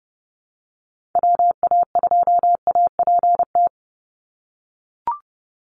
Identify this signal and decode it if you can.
Morse code, heard as a keyed tone, about 30 wpm, 700 Hz. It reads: WA2APT
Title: USA Call Signs 30wpm